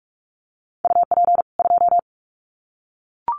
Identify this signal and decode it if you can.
Morse code, heard as a keyed tone, about 44 wpm, 700 Hz. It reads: UP 2
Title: Common QSO Phrases 45wpm